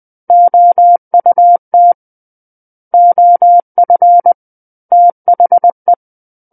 Morse code, heard as a keyed tone, about 20 wpm, 700 Hz. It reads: OUT OF THE
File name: 20 WPM